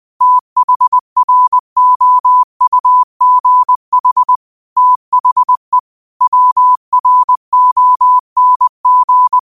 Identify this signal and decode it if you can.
Morse code, heard as a keyed tone, about 20 wpm, 1000 Hz. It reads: THROUGH THE WRONG
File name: qotdfiles / 20wpm